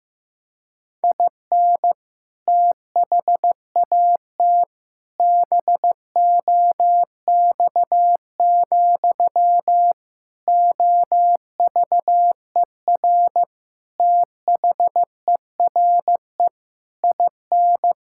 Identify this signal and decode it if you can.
Morse code, heard as a keyed tone, about 15 wpm, 700 Hz. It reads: IN THAT BOX, OVER THERE IN
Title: Sentences from Top 1600 Words 15wpm